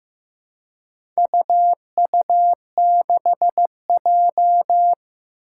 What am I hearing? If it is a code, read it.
Morse code, heard as a keyed tone, about 15 wpm, 700 Hz. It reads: UU6J
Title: Four Challenging Letter-Number Combinations 15wpm